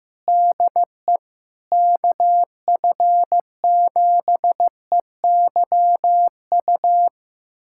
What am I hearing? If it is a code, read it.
Morse code, heard as a keyed tone, about 15 wpm, 700 Hz. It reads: DE KF7EYU
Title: USA Callsign DE Callsign 15wpm